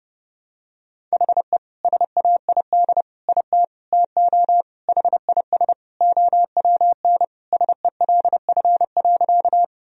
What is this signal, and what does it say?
Morse code, heard as a keyed tone, about 32 wpm, 700 Hz. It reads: HE SAID IT TO HIS OWN SELF.
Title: Sentences from Top 1000 Words - Review Part 1 - 30wpm